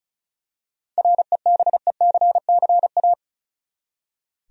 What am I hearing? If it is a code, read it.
Morse code, heard as a keyed tone, about 36 wpm, 700 Hz. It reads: REBECCA
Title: USA Common Names - 35wpm